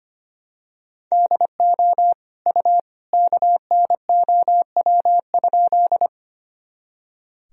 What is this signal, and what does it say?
Morse code, heard as a keyed tone, about 25 wpm, 700 Hz. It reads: DO U KNOW?
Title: Sentences from Top 100 Words and Abbreviations 25wpm